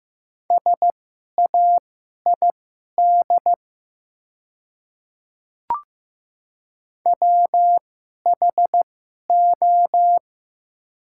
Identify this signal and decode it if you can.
Morse code, heard as a keyed tone, about 15 wpm, spaced as for 11 wpm, 690 Hz. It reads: SAID WHO
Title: 2x Character Spacing - Top 100 Words - 15wpm